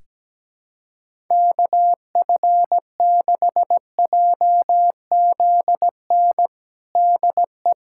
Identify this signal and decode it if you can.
Morse code, heard as a keyed tone, about 17 wpm, 700 Hz. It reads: KF6JZN DE
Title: USA Callsign DE Callsign - 6 Characters - 17wpm